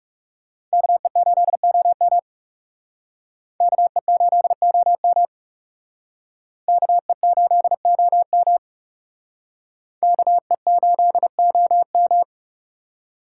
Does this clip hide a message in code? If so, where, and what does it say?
Yes: Morse code from 0.7 to 12.2 s, heard as a keyed tone, about 39 wpm, 700 Hz. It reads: KE8OM KE8OM KE8OM KE8OM